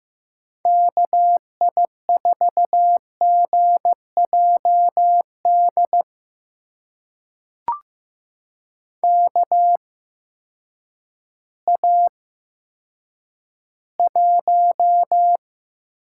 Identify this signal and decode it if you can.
Morse code, heard as a keyed tone, about 15 wpm, spaced as for 9 wpm, 700 Hz. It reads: KI4GJD KA1
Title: USA Call Signs - 6 Characters - 8x Character Spacing - 15wpm